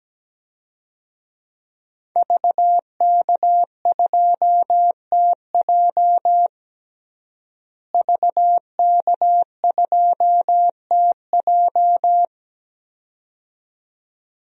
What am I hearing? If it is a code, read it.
Morse code, heard as a keyed tone, about 17 wpm, 700 Hz. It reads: VK2TJ VK2TJ
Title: DXCC-100 Call Signs - 5 Characters - Repeated 2x - 17wpm